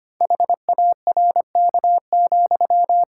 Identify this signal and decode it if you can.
Morse code, heard as a keyed tone, about 25 wpm, 700 Hz. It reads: HARK,
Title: Pangram Challenge v2 - 25wpm